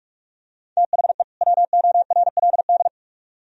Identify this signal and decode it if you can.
Morse code, heard as a keyed tone, about 44 wpm, 700 Hz. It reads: THE WORLD